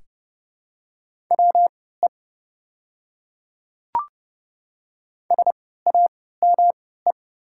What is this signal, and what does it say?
Morse code, heard as a keyed tone, about 29 wpm, spaced as for 17 wpm, 690 Hz. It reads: WE SAME
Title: 3x Character Spacing - Top 300 Words - 30wpm